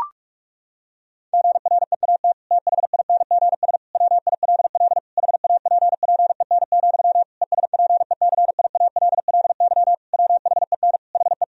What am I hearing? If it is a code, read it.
Morse code, heard as a keyed tone, about 46 wpm, 700 Hz. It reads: GREAT THINGS WILL HAPPEN, ESPECIALLY WHEN HE
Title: Sentences from Top 1000 Words - Review Part 4 - 45wpm